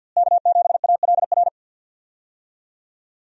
Morse code, heard as a keyed tone, about 48 wpm, 700 Hz. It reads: K7ALR
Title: USA Call Signs - 5 Characters - Repeated 2x - 50wpm